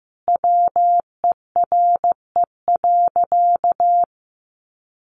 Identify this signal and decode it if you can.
Morse code, heard as a keyed tone, about 15 wpm, 700 Hz. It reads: WERE.
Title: Sentences from Top 1000 Words 15wpm